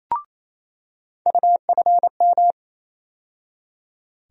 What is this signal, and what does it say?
Morse code, heard as a keyed tone, about 28 wpm, 700 Hz. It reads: UFM
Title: USA Call Sign Postfixes - 3 Letters 28wpm